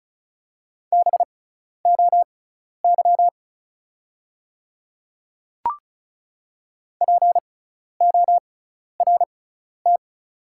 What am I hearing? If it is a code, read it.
Morse code, heard as a keyed tone, about 36 wpm, spaced as for 12 wpm, 700 Hz. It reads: BOY PORT
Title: 6x Character Spacing - Top 300 Words - 35wpm